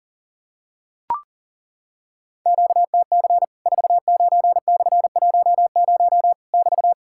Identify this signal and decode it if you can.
Morse code, heard as a keyed tone, about 40 wpm, 700 Hz. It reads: QTC 49/10 <BT>
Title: Worked All Europe DX Contest - QTC Exchange - 40wpm